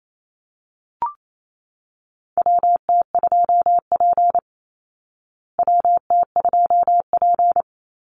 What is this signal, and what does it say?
Morse code, heard as a keyed tone, about 28 wpm, 700 Hz. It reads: WT2P WT2P
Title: Call Signs Repeated 2x - 28wpm